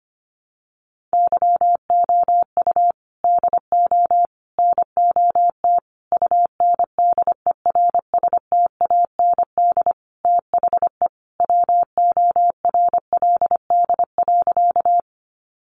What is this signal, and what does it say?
Morse code, heard as a keyed tone, about 25 wpm, 700 Hz. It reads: YOU DO NOT UNDERSTAND THE WORLD.
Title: Sentences from Top 600 Words 25wpm